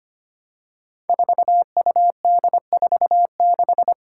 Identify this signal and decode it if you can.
Morse code, heard as a keyed tone, about 25 wpm, 700 Hz. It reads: 4UD46
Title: Five Challenging Letter-Number Combinations 25wpm